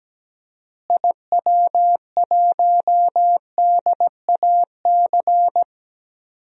Morse code, heard as a keyed tone, about 17 wpm, 700 Hz. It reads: IW1DAC
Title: DXCC-100 Call Signs - 6 Characters - Repeated 2x - 17wpm